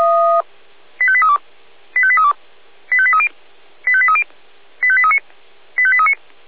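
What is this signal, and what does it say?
Five-tone selective call (ZVEI), heard as quick runs of five tones, tones of about 70 ms. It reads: 85832 85832 85830 85830 85839 85839
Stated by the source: ZVEI five-tone selective call, received on SDR.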